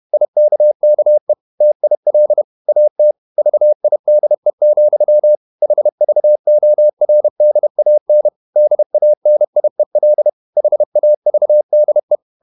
Morse code, heard as a keyed tone, about 31 wpm, 600 Hz. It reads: IKKE TIL AT VIDE, HVORDAN DANIEL HAVDE
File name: Ferie QRQ 31wpm